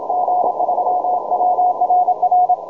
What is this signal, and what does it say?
Morse code, heard as a keyed tone, about 28 wpm, 720 Hz, with noise. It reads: G3WRR
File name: juma-trx2-80m-cw-sample